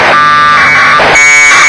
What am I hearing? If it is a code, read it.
Packet radio (AX.25, as in APRS).
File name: vk3ukf / ISS packet QSO